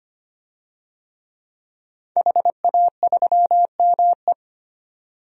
Morse code, heard as a keyed tone, about 25 wpm, 700 Hz. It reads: HA3ME
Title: DXCC-100 Call Signs - 5 Characters - 25wpm